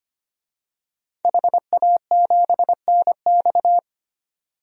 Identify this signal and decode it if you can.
Morse code, heard as a keyed tone, about 25 wpm, 700 Hz. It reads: HA7NX
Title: DXCC-100 Call Signs - 5 Characters - 25wpm